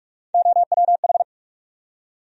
Morse code, heard as a keyed tone, about 46 wpm, 700 Hz. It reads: OWH
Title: USA Call Sign Postfixes - 3 Letters 45wpm